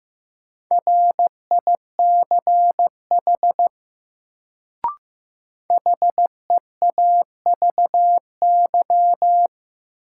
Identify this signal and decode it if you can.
Morse code, heard as a keyed tone, about 15 wpm, 700 Hz. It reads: RICH HEAVY
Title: Sentences from Top 1100 Words - New Word Drill Alt - 15wpm